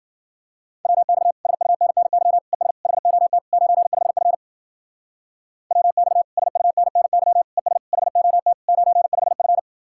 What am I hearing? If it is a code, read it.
Morse code, heard as a keyed tone, about 52 wpm, 700 Hz. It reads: WX SUNNY ES HOT 95F WX SUNNY ES HOT 95F